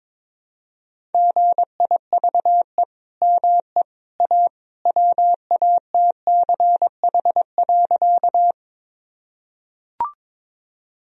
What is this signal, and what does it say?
Morse code, heard as a keyed tone, about 22 wpm, 700 Hz. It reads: GIVE ME A WATCH.
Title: Sentences from Top 1000 Words - Review Part 1 - 22wpm